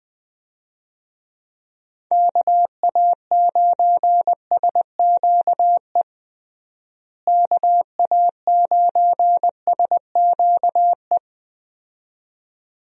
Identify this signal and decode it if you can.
Morse code, heard as a keyed tone, about 20 wpm, 700 Hz. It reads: KA9SQE KA9SQE
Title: USA Call Signs - 6 Characters - Repeated 2x - 20wpm